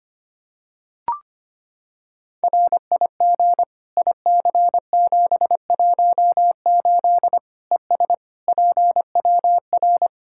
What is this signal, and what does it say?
Morse code, heard as a keyed tone, about 25 wpm, 700 Hz. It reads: RIG IC718 ES PWR
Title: Practice QSO - Exchange 2 - Info - 25wpm